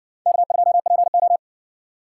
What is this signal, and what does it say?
Morse code, heard as a keyed tone, about 55 wpm, 700 Hz. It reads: K2PO